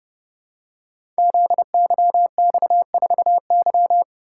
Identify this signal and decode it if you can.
Morse code, heard as a keyed tone, about 30 wpm, 700 Hz. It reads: ZYX4Y
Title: Code Groups - 5 Letter:Numbers - 30wpm